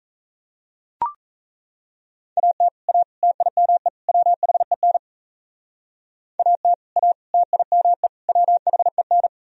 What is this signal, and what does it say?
Morse code, heard as a keyed tone, about 34 wpm, 700 Hz. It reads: AT A TIME WHEN AT A TIME WHEN
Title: Speed-Racing - Sets of 4 Words 28wpm